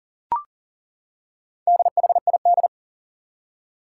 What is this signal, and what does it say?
Morse code, heard as a keyed tone, about 40 wpm, 700 Hz. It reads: DHID
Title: Four Challenging Letter-Number Combinations 40wpm